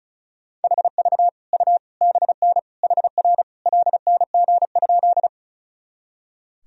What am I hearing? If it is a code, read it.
Morse code, heard as a keyed tone, about 34 wpm, 700 Hz. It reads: HV U BN HR LNG?
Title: Sentences from Top 200 Words and Abbreviations 35wpm